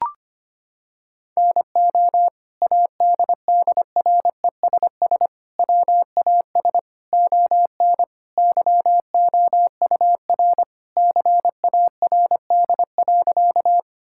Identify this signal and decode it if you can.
Morse code, heard as a keyed tone, about 25 wpm, 700 Hz. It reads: NO ADDRESS WAS ON YOUR CARD.
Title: Sentences from Top 1000 Words - Review Part 6 - 25wpm